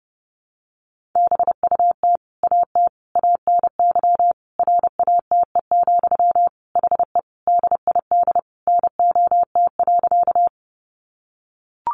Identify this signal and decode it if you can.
Morse code, heard as a keyed tone, about 30 wpm, 700 Hz. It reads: BUT AT ANY RATE, HE DID NOT.
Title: Sentences from Top 700 Words 30wpm